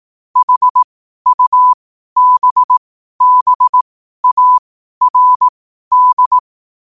Morse code, heard as a keyed tone, about 18 wpm, spaced as for 12 wpm, 1000 Hz. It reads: HUBBARD
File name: qotdfiles / 12wpm